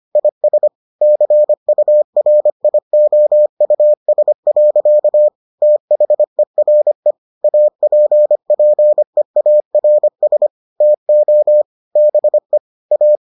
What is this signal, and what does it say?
Morse code, heard as a keyed tone, about 25 wpm, 600 Hz. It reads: IS CURIOUS. THERE APPEARS TO BE A